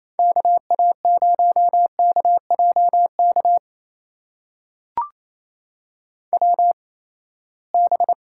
Morse code, heard as a keyed tone, about 28 wpm, 700 Hz. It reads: K A 0 K J K W B
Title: USA Call Signs - 6 Characters - 8x Character Spacing - 28wpm